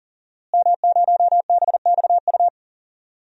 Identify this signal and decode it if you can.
Morse code, heard as a keyed tone, about 40 wpm, 700 Hz. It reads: M0BXU
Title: DXCC-100 Call Signs - 5 Characters - Repeated 3x - 40wpm